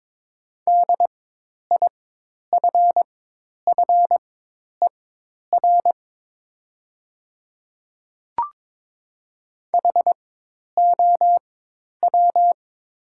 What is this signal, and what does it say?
Morse code, heard as a keyed tone, about 22 wpm, spaced as for 10 wpm, 690 Hz. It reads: DIFFER HOW
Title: 4x Character Spacing - Top 200 Words - 22wpm